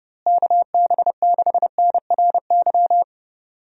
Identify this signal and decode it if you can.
Morse code, heard as a keyed tone, about 32 wpm, 700 Hz. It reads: KB6NRY
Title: USA Call Signs - 6 Characters - Repeated 3x - 30wpm